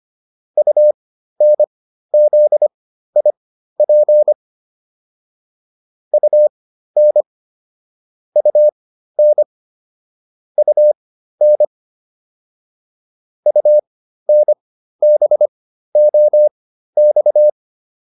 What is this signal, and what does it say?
Morse code, heard as a keyed tone, about 25 wpm, spaced as for 13 wpm, 600 Hz. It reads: UNZIP UN UN UN UNBOX
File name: UN5 13